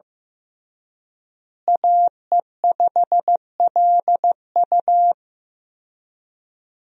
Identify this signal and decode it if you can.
Morse code, heard as a keyed tone, about 15 wpm, 700 Hz. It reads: AE5LU
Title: USA Call Signs - 5 Characters - Repeated 3x - 15wpm